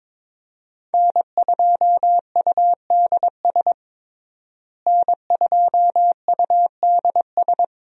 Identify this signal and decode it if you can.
Morse code, heard as a keyed tone, about 21 wpm, 700 Hz. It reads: N2UDS N2UDS
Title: USA Call Signs - 5 Characters - Repeated 2x - 22wpm